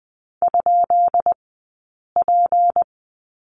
Morse code, heard as a keyed tone, about 20 wpm, spaced as for 8 wpm, 700 Hz. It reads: ?P
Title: 197 Lesson 32d - Letter k - Two Characters 20wpm